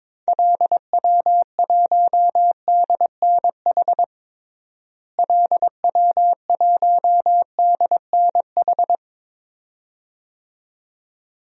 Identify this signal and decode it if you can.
Morse code, heard as a keyed tone, about 22 wpm, 700 Hz. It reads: LW1DNH LW1DNH
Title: DXCC-100 Call Signs - 6 Characters - Repeated 2x - 22wpm